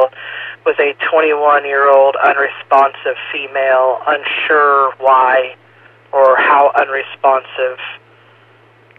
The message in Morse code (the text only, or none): none